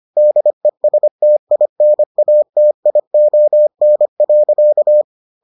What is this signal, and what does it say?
Morse code, heard as a keyed tone, about 25 wpm, 600 Hz. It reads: DESTINATION.